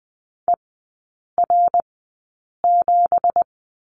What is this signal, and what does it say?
Morse code, heard as a keyed tone, about 20 wpm, spaced as for 8 wpm, 700 Hz. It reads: ER7
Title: 212 Lesson 34e - Number 8 - Three Characters 20wpm